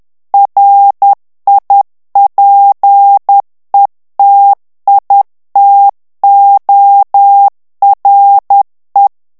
Morse code, heard as a keyed tone, about 10 wpm, 800 Hz. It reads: RIPETITORE
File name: morse mp3 / ITALIEN texte clair 1 vitesse 10 mots minute